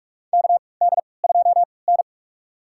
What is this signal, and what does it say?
Morse code, heard as a keyed tone, about 44 wpm, spaced as for 25 wpm, 700 Hz. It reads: KD2N
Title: USA Call Signs - 4 Characters - 3x Character Spacing - 45wpm